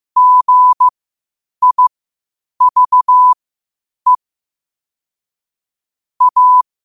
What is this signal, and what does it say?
Morse code, heard as a keyed tone, about 15 wpm, spaced as for 8 wpm, 1000 Hz. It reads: GIVE A